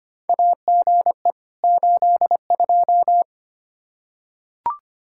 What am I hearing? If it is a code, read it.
Morse code, heard as a keyed tone, about 25 wpm, 700 Hz. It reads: AGE 82
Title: Practice QSO - Exchange 3 - More Info - 25wpm